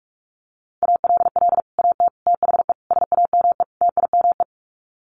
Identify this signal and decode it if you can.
Morse code, heard as a keyed tone, about 44 wpm, 700 Hz. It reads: ALL AT THE SAME TIME
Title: Sets of 5 Words Encore2 45wpm